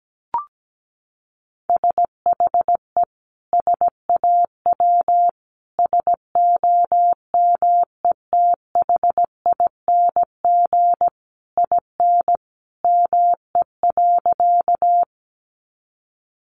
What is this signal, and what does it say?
Morse code, heard as a keyed tone, about 17 wpm, 700 Hz. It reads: SHE SAW SOMETHING IN ME.
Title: Sentences from Top 600 Words 17wpm